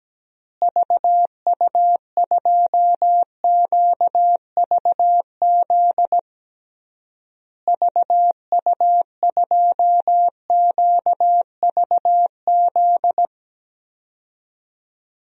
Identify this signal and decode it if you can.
Morse code, heard as a keyed tone, about 17 wpm, 700 Hz. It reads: VU2QVZ VU2QVZ
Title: DXCC-100 Call Signs - 6 Characters - Repeated 3x - 17wpm